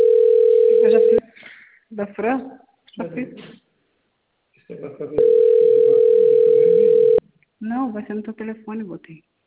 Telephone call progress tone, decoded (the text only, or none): ringback tone